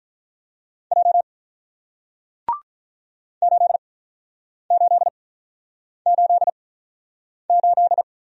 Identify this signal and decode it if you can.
Morse code, heard as a keyed tone, about 48 wpm, spaced as for 9 wpm, 700 Hz. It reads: J 8888